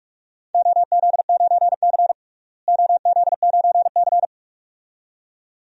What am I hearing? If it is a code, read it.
Morse code, heard as a keyed tone, about 46 wpm, 700 Hz. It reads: OZ9C OZ9C
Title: DXCC-100 Call Signs - 4 Characters - Repeated 2x - 45wpm